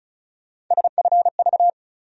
Morse code, heard as a keyed tone, about 36 wpm, 700 Hz. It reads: SFV
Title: USA Call Sign Postfixes - 3 Letters 35wpm